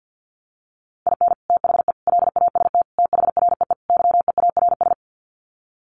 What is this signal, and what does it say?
Morse code, heard as a keyed tone, about 50 wpm, 700 Hz. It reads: IN THE LAST THREE YEARS
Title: Sets of 5 Words Encore2 50wpm